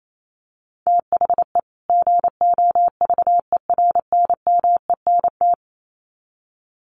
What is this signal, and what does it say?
Morse code, heard as a keyed tone, about 28 wpm, 700 Hz. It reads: THE GOVERNMENT
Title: Sets of 2 Words 28wpm